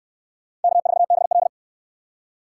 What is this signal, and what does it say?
Morse code, heard as a keyed tone, about 67 wpm, 700 Hz. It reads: K4BL